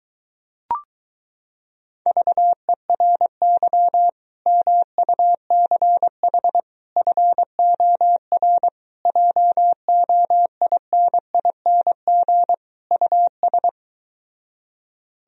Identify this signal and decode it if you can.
Morse code, heard as a keyed tone, about 23 wpm, 700 Hz. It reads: VERY MUCH FOR JOINING US